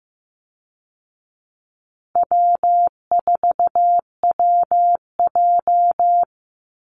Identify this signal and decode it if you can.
Morse code, heard as a keyed tone, about 14 wpm, 700 Hz. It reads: W4WJ
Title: Easy Call Signs Repeated 2x - 15wpm